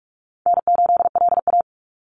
Morse code, heard as a keyed tone, about 46 wpm, 700 Hz. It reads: N8LA
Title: Easy Call Signs - 45wpm